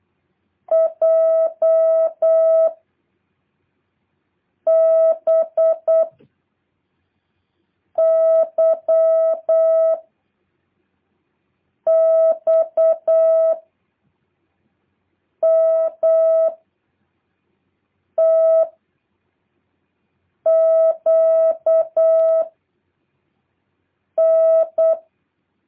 Morse code, heard as a keyed tone, about 8 wpm, 650 Hz. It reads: J B Y X M T Q N
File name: jh8cbh / 2023cw